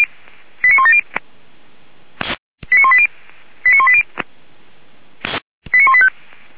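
Five-tone selective call (ZVEI), heard as quick runs of five tones, tones of about 70 ms. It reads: 89170 89180 89180 89185
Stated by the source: ZVEI five-tone selective call received by SDR, from the feuerwehr-orlamuende.org fire brigade audio server.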